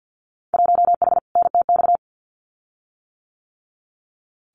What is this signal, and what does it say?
Morse code, heard as a keyed tone, about 50 wpm, 700 Hz. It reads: 1H NTX
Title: Winter Field Day Exchanges 50wpm